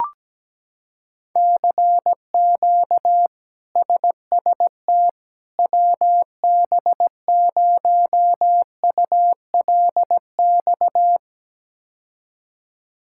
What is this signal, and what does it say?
Morse code, heard as a keyed tone, about 17 wpm, 700 Hz. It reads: CQ SST WB0ULX
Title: K1USN SST QSO Practice - 17wpm